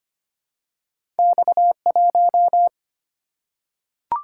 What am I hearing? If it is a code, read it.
Morse code, heard as a keyed tone, about 25 wpm, 700 Hz. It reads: X 1
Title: Single-Letter Single-Number 25wpm